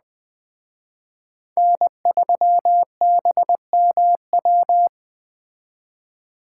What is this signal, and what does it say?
Morse code, heard as a keyed tone, about 20 wpm, 700 Hz. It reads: N3BMW
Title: Speed Racing - USA 1x3 Call Signs 17wpm